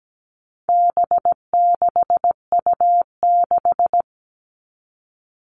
Morse code, heard as a keyed tone, about 17 wpm, 700 Hz. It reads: B6U6
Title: All Cut Numbers - 4-Digit 17wpm